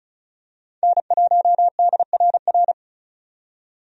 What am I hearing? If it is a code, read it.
Morse code, heard as a keyed tone, about 36 wpm, 700 Hz. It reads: N1DRR
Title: USA Call Signs - 5 Characters - 35wpm